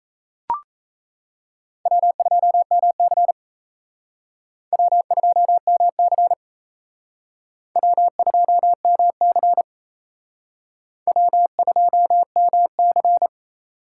Morse code, heard as a keyed tone, about 35 wpm, 700 Hz. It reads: W2MC W2MC W2MC W2MC